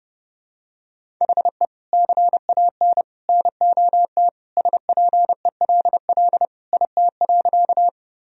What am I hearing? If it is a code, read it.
Morse code, heard as a keyed tone, about 31 wpm, 700 Hz. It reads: HE CAN NOT SPELL IT.
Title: Sentences from Top 1000 Words - Review Part 1 - 30wpm